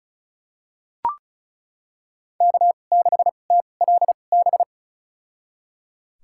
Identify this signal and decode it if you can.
Morse code, heard as a keyed tone, about 34 wpm, spaced as for 25 wpm, 700 Hz. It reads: K6TLB
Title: USA Call Signs - 2x Character Spacing - 35wpm